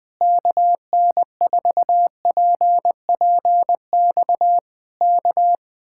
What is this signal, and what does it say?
Morse code, heard as a keyed tone, about 20 wpm, 700 Hz. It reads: KN4PPX K
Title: Speed-Racing Common QSO Phrases - 20wpm